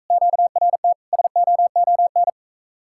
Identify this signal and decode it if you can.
Morse code, heard as a keyed tone, about 44 wpm, 700 Hz. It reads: QRT SOON